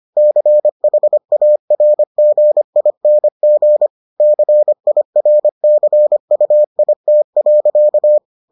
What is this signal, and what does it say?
Morse code, heard as a keyed tone, about 25 wpm, 600 Hz. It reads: CHARGING CIRCUIT.